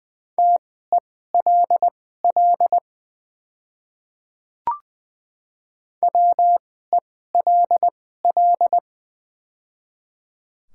Morse code, heard as a keyed tone, about 20 wpm, spaced as for 15 wpm, 690 Hz. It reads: TELL WELL
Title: 2x Character Spacing - Top 200 Words - 20wpm